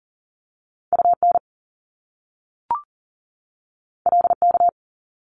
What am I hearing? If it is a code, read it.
Morse code, heard as a keyed tone, about 42 wpm, 700 Hz. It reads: UN LK
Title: Top Two Letter Combinations in Words 40wpm